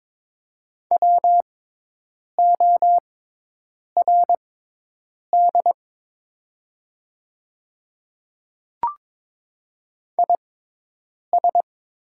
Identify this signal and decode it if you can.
Morse code, heard as a keyed tone, about 23 wpm, spaced as for 8 wpm, 700 Hz. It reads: WORD IS
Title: 6x Character Spacing - Top 100 Words - 22wpm